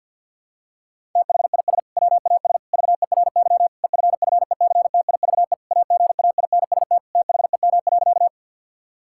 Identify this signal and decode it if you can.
Morse code, heard as a keyed tone, about 52 wpm, 700 Hz. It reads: THIS WAS VERY EFFECTIVE AGAINST THEM.
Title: Sentences from Top 1000 Words - Review Part 6 - 50wpm